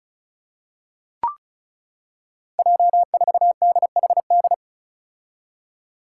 Morse code, heard as a keyed tone, about 34 wpm, 700 Hz. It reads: J4DHD
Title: Five Challenging Letter-Number Combinations 35wpm